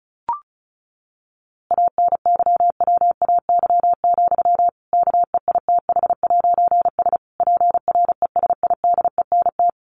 Morse code, heard as a keyed tone, about 34 wpm, 700 Hz. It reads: ANYWAY, KEITH'S PRESIDENT
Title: A Sound Of Thunder 35wpm